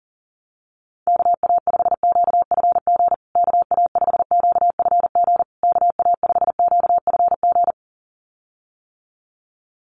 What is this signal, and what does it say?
Morse code, heard as a keyed tone, about 40 wpm, 700 Hz. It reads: KA5QFG KA5QFG KA5QFG
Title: USA Call Signs Repeated 3x 40wpm